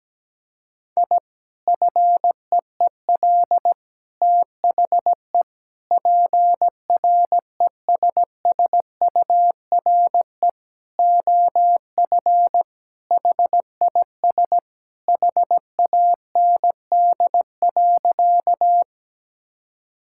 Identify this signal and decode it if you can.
Morse code, heard as a keyed tone, about 17 wpm, 700 Hz. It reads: I FEEL THE PRESSURE OF HIS HAND.
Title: Sentences from Top 1000 Words - Review Part 6 - 17wpm